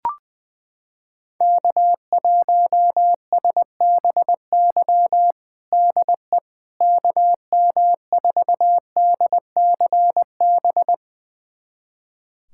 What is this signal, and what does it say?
Morse code, heard as a keyed tone, about 21 wpm, 700 Hz. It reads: K1SBY DE KM4DCB
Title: USA Callsign DE Callsign 20wpm